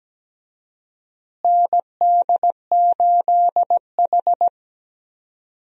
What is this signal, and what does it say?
Morse code, heard as a keyed tone, about 17 wpm, 700 Hz. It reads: ND8H
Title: USA Call Signs - 4 Characters - Repeated 3x - 17wpm